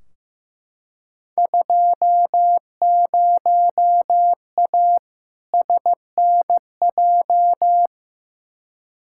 Morse code, heard as a keyed tone, about 15 wpm, 700 Hz. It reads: 20A SNJ
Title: Field Day Exchanges Encore - 15wpm